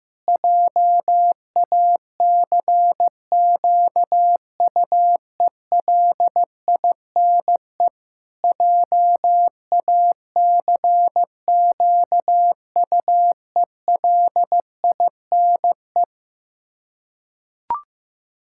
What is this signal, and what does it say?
Morse code, heard as a keyed tone, about 16 wpm, 700 Hz. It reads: JACQUELINE JACQUELINE
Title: K1USN SST QSO Practice 15wpm